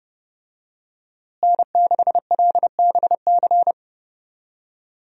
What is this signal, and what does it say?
Morse code, heard as a keyed tone, about 30 wpm, 700 Hz. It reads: N6LBC
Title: USA Call Signs - 5 Characters - Repeated 3x - 30wpm